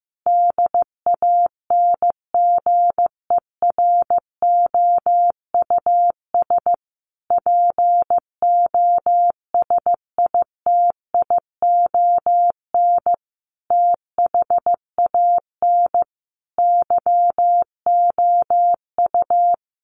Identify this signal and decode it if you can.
Morse code, heard as a keyed tone, about 15 wpm, 700 Hz. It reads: DANGEROUS POSITION THAN YOU
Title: Sentences from Top 1400 Words 15wpm